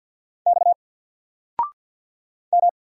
Morse code, heard as a keyed tone, about 50 wpm, 700 Hz. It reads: X M
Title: Single Letter-Number 50wpm